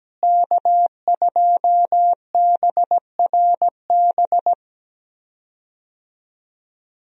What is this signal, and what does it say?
Morse code, heard as a keyed tone, about 17 wpm, 700 Hz. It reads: K2BRB
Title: USA Call Signs - 5 Characters - Repeated 2x - 17wpm